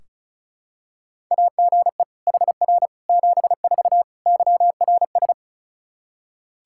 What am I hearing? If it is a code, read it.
Morse code, heard as a keyed tone, about 34 wpm, 700 Hz. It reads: AGE HR 74 YRS
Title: Practice QSO - Exchange 3 - More Info - 35wpm